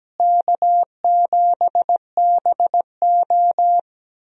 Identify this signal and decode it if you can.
Morse code, heard as a keyed tone, about 17 wpm, 700 Hz. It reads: K7BO
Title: USA Call Signs - 4 Characters - Repeated 3x - 17wpm